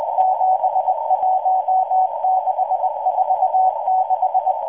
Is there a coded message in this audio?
Yes: Morse code from 0.2 to 4.5 s, heard as a keyed tone, about 20 wpm, 750 Hz, with noise. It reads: 71646